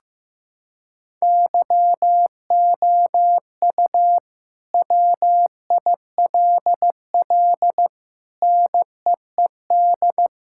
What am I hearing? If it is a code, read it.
Morse code, heard as a keyed tone, about 15 wpm, 700 Hz. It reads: YOU WILL NEED
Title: Sentences from Top 1000 Words - Review Part 5 - 15wpm